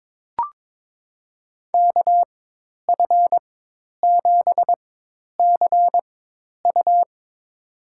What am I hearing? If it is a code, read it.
Morse code, heard as a keyed tone, about 22 wpm, spaced as for 10 wpm, 700 Hz. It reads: KF7CU
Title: USA Call Signs - 5 Characters - 4x Character Spacing - 22wpm